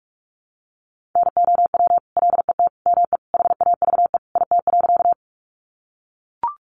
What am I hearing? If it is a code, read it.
Morse code, heard as a keyed tone, about 44 wpm, 700 Hz. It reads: NOW LET ME HAVE IT.
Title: Sentences from Top 300 Words 45wpm